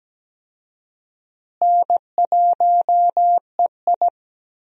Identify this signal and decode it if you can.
Morse code, heard as a keyed tone, about 17 wpm, 700 Hz. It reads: N1EI
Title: USA Call Signs - 4 Characters - Repeated 2x - 17wpm